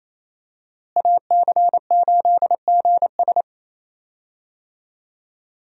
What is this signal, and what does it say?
Morse code, heard as a keyed tone, about 28 wpm, 700 Hz. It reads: AC8GS
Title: USA Call Signs - 5 Characters - Repeated 2x - 28wpm